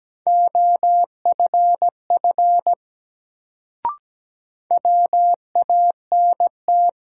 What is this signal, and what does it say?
Morse code, heard as a keyed tone, about 17 wpm, 700 Hz. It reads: OFF WANT
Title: Sentences from Top 200 Words - New Word Drill - 17wpm